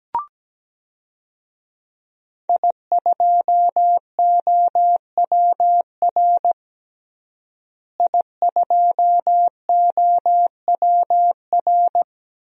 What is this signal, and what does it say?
Morse code, heard as a keyed tone, about 17 wpm, 700 Hz. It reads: I2OWR I2OWR
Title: DXCC-100 Call Signs - 5 Characters - Repeated 2x - 17wpm